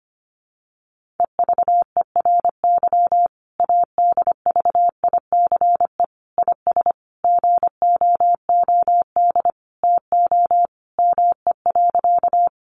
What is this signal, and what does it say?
Morse code, heard as a keyed tone, about 25 wpm, 700 Hz. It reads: EVERY ADVICE IS GOOD TO ME.
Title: Sentences from Top 1100 Words 25wpm